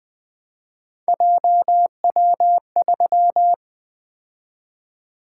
Morse code, heard as a keyed tone, about 20 wpm, 700 Hz. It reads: JW3
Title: Three Letter-Number Combinations - 20wpm